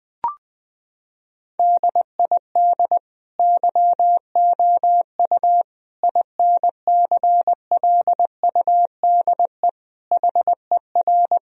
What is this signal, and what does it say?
Morse code, heard as a keyed tone, about 20 wpm, 700 Hz. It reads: DID YOU INCLUDE HER
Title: Sentences from Top 1000 Words - Review Part 3 - 20wpm